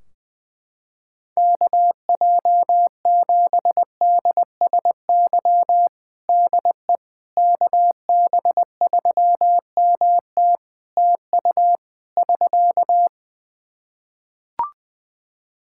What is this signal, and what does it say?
Morse code, heard as a keyed tone, about 20 wpm, 700 Hz. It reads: KJ7DSY DE KB3MT TU <SK>
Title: K1USN SST QSO Practice 20wpm